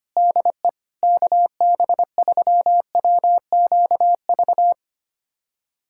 Morse code, heard as a keyed tone, about 25 wpm, 700 Hz. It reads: DE KB3WQV
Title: USA Callsign DE Callsign 25wpm